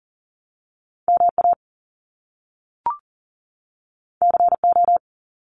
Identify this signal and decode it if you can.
Morse code, heard as a keyed tone, about 40 wpm, 700 Hz. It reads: MA CO
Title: US States 40wpm